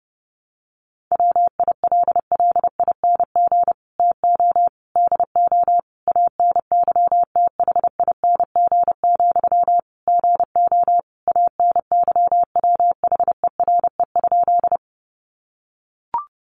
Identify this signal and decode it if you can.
Morse code, heard as a keyed tone, about 30 wpm, 700 Hz. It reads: WILLING TO DO ANYTHING, GO ANYWHERE?
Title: Sentences from Top 1400 Words 30wpm